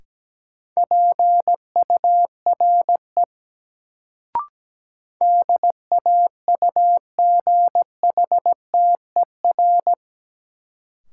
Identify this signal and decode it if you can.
Morse code, heard as a keyed tone, about 17 wpm, 700 Hz. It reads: PURE DAUGHTER
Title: Sentences from Top 1500 Words - New Word Drill - 17wpm